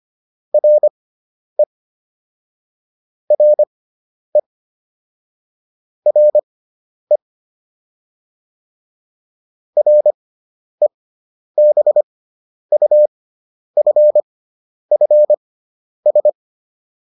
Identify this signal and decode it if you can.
Morse code, heard as a keyed tone, about 25 wpm, spaced as for 10 wpm, 600 Hz. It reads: RE RE RE REBUFFS